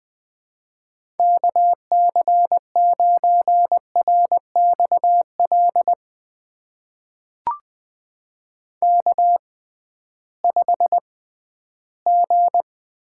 Morse code, heard as a keyed tone, about 20 wpm, spaced as for 13 wpm, 700 Hz. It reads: KC9RXL K5G
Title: USA Call Signs - 6x Character Spacing - 20wpm